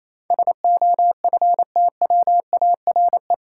Morse code, heard as a keyed tone, about 28 wpm, 700 Hz. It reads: SOFTWARE
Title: Sentences from Top 1000 Words - Review Part 3 - New Word Drill - 28wpm